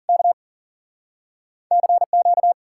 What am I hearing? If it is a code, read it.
Morse code, heard as a keyed tone, about 53 wpm, 700 Hz. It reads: K CQ